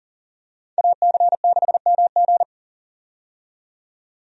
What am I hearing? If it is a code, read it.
Morse code, heard as a keyed tone, about 40 wpm, 700 Hz. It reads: AC6MG
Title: USA Call Signs - 5 Characters - 40wpm